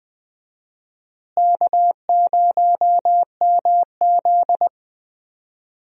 Morse code, heard as a keyed tone, about 20 wpm, 700 Hz. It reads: K0MZ
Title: USA Call Signs - 4 Characters - Repeated 3x - 20wpm